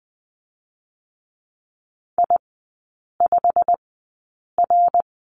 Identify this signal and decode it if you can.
Morse code, heard as a keyed tone, about 20 wpm, spaced as for 8 wpm, 700 Hz. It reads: I5R
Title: 198 Lesson 32e - Letter k - Three Characters 20wpm